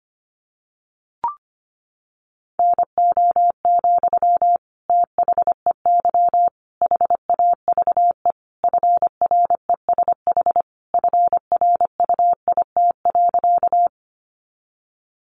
Morse code, heard as a keyed tone, about 25 wpm, 700 Hz. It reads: NO, THEY HAVE FRESH FRUIT.
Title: Sentences from Top 1300 Words 25wpm